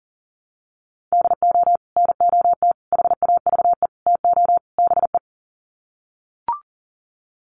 Morse code, heard as a keyed tone, about 37 wpm, 700 Hz. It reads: DO NOT HAVE TO BE
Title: Sets of 5 Words 40wpm